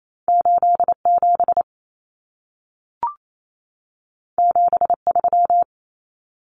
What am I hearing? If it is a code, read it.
Morse code, heard as a keyed tone, about 28 wpm, 700 Hz. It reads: 87 73
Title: Common Cut Numbers - 2-Digit 28wpm